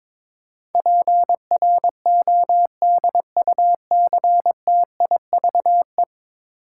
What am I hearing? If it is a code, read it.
Morse code, heard as a keyed tone, about 22 wpm, 700 Hz. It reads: PRODUCTIVE
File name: practice / 22wpm / 10 Letter Words - First 3 Letters Given - 22wpm